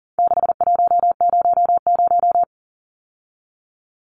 Morse code, heard as a keyed tone, about 37 wpm, 700 Hz. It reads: 6100
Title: Numbers - 4-Digit 40wpm